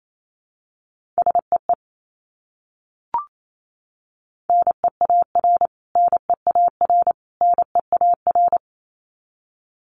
Morse code, heard as a keyed tone, about 28 wpm, 700 Hz. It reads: SEE NEAR NEAR NEAR
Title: Top 300 Words Repeated 28wpm